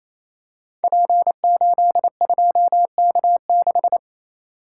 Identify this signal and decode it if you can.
Morse code, heard as a keyed tone, about 29 wpm, 700 Hz. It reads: P82K6
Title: Code Groups - 5 Letter:Numbers - 28wpm